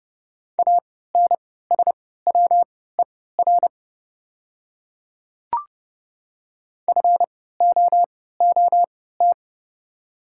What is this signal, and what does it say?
Morse code, heard as a keyed tone, about 30 wpm, spaced as for 17 wpm, 700 Hz. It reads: ANSWER FOOT
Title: 3x Character Spacing - Top 500 Words - 30wpm